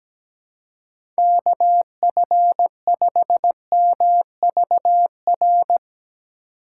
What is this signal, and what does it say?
Morse code, heard as a keyed tone, about 17 wpm, 700 Hz. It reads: KF5MVR
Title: USA Call Signs - 6 Characters - 4x Character Spacing - 17wpm